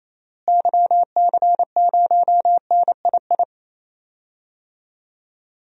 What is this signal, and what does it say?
Morse code, heard as a keyed tone, about 28 wpm, 700 Hz. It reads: YC0NII
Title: DXCC-100 Call Signs - 6 Characters - Repeated 2x - 28wpm